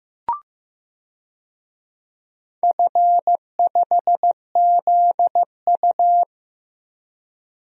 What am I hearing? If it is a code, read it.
Morse code, heard as a keyed tone, about 15 wpm, 700 Hz. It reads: F5ZU
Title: DXCC-100 Call Signs - 4 Characters - 15wpm